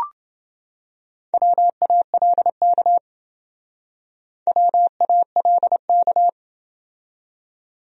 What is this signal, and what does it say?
Morse code, heard as a keyed tone, about 28 wpm, 700 Hz. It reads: WALK WALK